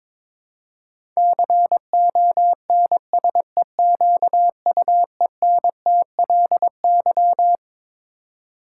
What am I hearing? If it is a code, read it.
Morse code, heard as a keyed tone, about 22 wpm, 700 Hz. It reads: CONSEQUENTLY
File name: Twelve Letter Words - 22wpm